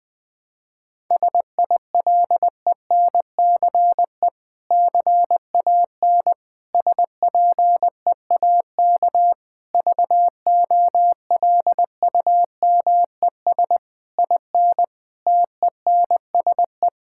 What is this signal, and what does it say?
Morse code, heard as a keyed tone, about 20 wpm, 700 Hz. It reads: SILENCE CAN SPEAK VOLUMES IN TENSE
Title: Difficult Sentences with Hints - 20wpm